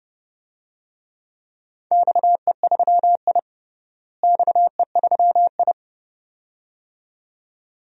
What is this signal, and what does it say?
Morse code, heard as a keyed tone, about 30 wpm, 700 Hz. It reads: XE3I XE3I
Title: DXCC-100 Call Signs - 4 Characters - Repeated 2x - 30wpm